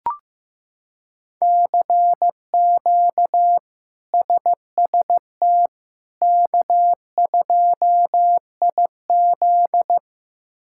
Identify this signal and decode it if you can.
Morse code, heard as a keyed tone, about 15 wpm, 700 Hz. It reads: CQ SST K2IZ
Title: K1USN SST QSO Practice - 15wpm